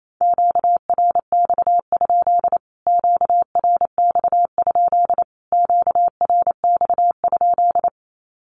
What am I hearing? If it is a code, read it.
Morse code, heard as a keyed tone, about 28 wpm, 700 Hz. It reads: QRX? QRX? QRX?
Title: Q-Codes Repeated 3x 28wpm